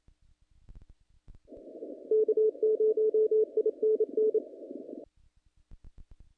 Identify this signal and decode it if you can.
Morse code, heard as a keyed tone, about 27 wpm, 430 Hz. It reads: K0INN
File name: contests / naqp-cw-k5tr-op-k5pi- -2100